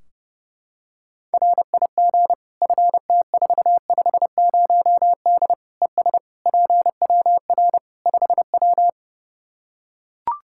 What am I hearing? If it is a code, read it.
Morse code, heard as a keyed tone, about 30 wpm, 700 Hz. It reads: RIG FT450D ES PWR 5W
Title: Practice QSO - Exchange 2 - Info - 30wpm